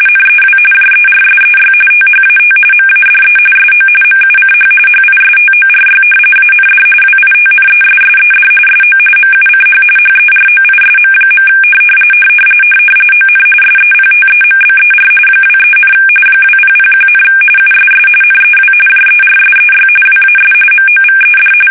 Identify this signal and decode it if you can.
RTTY, about 45.45 baud. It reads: NEW JERSEY THAT MADE SO MUCH HISTORY AND WITNESSED SO MUCH HEROISM. ON SEPTEMBER 23, 2001, USS NEW JERSEY WAS MOVED
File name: fsk850 / battleshipnj